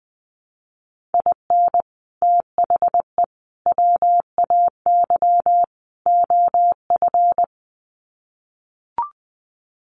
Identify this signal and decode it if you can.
Morse code, heard as a keyed tone, about 20 wpm, 700 Hz. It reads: IN THE WAY OF
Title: Sets of 4 Words 20wpm